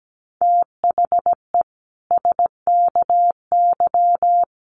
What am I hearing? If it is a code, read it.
Morse code, heard as a keyed tone, about 17 wpm, 700 Hz. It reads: THE SKY
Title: Sentences from Top 1200 Words 17wpm